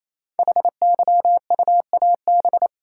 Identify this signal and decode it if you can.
Morse code, heard as a keyed tone, about 28 wpm, 700 Hz. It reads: HYUAB
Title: Code Groups - 5 Letter:Numbers - 28wpm